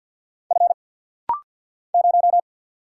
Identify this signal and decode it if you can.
Morse code, heard as a keyed tone, about 50 wpm, 700 Hz. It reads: F 0
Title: Single Letter-Number - Rapid-Fire 50wpm